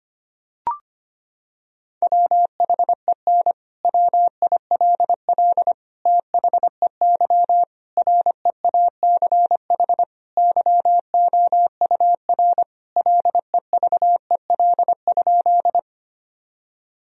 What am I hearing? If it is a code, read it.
Morse code, heard as a keyed tone, about 25 wpm, 700 Hz. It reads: WHEN WILL THEY REACH YOUR LEVEL?
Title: Sentences from Top 1000 Words - Review Part 4 - 25wpm